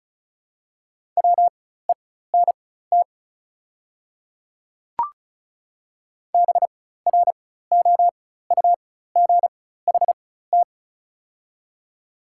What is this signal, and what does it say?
Morse code, heard as a keyed tone, about 36 wpm, spaced as for 16 wpm, 690 Hz. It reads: WENT BROUGHT
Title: 4x Character Spacing - Top 500 Words - 35wpm